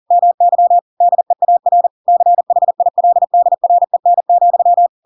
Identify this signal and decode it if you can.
Morse code, heard as a keyed tone, about 40 wpm, 700 Hz. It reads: MY DEAR CHILDREN,
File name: Pinocchio / Pinocchio ch4 40 40 ik1hge